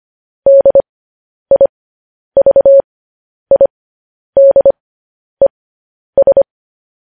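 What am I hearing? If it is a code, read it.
Morse code, heard as a keyed tone, about 25 wpm, spaced as for 10 wpm, 550 Hz. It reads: DIVIDES
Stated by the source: News headlines sent in Morse code at 10 wpm.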